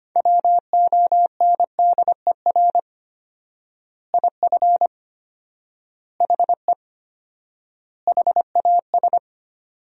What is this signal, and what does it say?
Morse code, heard as a keyed tone, about 25 wpm, 700 Hz. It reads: WONDER IF HE HAS
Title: 4x Word-Spacing - Sentences from Top 500 Words - 25wpm